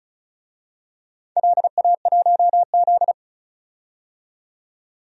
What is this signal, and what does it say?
Morse code, heard as a keyed tone, about 36 wpm, 690 Hz. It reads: LA1Z
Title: DXCC-100 Call Signs - 4 Characters - 35wpm